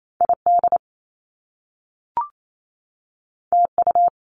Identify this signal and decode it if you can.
Morse code, heard as a keyed tone, about 28 wpm, 700 Hz. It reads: ID TU
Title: Top Two Letter Combinations in Words 28wpm